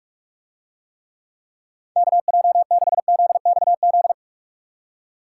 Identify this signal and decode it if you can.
Morse code, heard as a keyed tone, about 46 wpm, 700 Hz. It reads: KJ6ZXZ